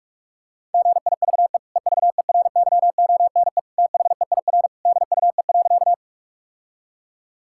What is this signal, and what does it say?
Morse code, heard as a keyed tone, about 44 wpm, 700 Hz. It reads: GIVE EVERYONE THEIR DUE.
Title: Sentences from Top 1000 Words - Review Part 6 - 45wpm